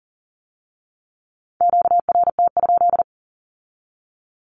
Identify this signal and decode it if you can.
Morse code, heard as a keyed tone, about 42 wpm, 700 Hz. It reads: QRT?
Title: Q-Codes with Spoken Meaning - 40wpm